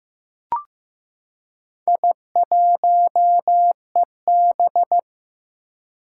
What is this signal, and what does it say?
Morse code, heard as a keyed tone, about 15 wpm, 700 Hz. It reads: I1EB
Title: Four Challenging Letter-Number Combinations 15wpm